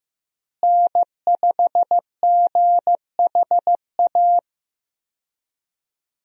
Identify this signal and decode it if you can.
Morse code, heard as a keyed tone, about 15 wpm, 700 Hz. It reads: N5GHA
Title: USA Call Signs - 5 Characters - Repeated 2x - 15wpm